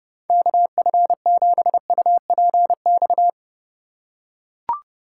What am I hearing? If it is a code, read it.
Morse code, heard as a keyed tone, about 30 wpm, 700 Hz. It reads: KF7UPX
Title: USA Call Signs - 6 Characters - 2x Character Spacing - 30wpm